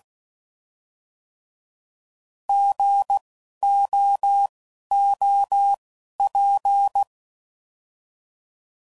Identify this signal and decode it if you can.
Morse code, heard as a keyed tone, about 16 wpm, spaced as for 11 wpm, 800 Hz. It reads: GOOP
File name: Prefixes / By First Letter